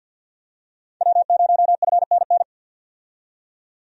Morse code, heard as a keyed tone, about 49 wpm, 700 Hz. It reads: W0FNN
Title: USA Call Signs - 5 Characters - 50wpm